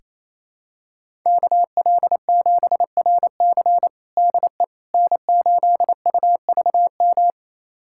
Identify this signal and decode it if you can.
Morse code, heard as a keyed tone, about 28 wpm, 700 Hz. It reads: KL7RC DE N8UVM
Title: USA Callsign DE Callsign - 5 Characters - 28wpm